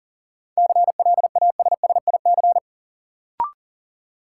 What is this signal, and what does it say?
Morse code, heard as a keyed tone, about 40 wpm, 700 Hz. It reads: CLASSIC
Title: Sentences from Top 1000 Words - New Word Drill - 40wpm